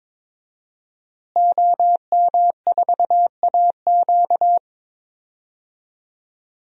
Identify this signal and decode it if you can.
Morse code, heard as a keyed tone, about 22 wpm, 700 Hz. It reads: OM4AQ
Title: DXCC-100 Call Signs - 5 Characters - Repeated 3x - 22wpm